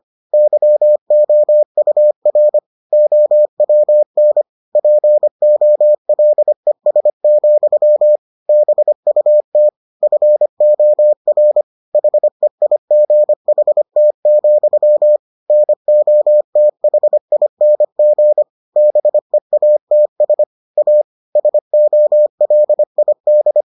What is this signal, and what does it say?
Morse code, heard as a keyed tone, about 25 wpm, 600 Hz. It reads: YOUR OWN POLES, BUT FOR HEIGHT, NOTHING BEATS A SOLID